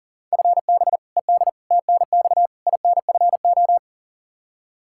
Morse code, heard as a keyed tone, about 40 wpm, 700 Hz. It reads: FB ED TNX INFO
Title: Practice QSO - Exchange 3 - More Info - 40wpm